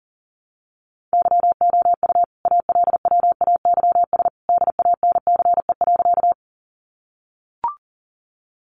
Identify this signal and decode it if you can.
Morse code, heard as a keyed tone, about 40 wpm, 700 Hz. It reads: YOU ALWAYS DANCE.
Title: Sentences from Top 700 Words 40wpm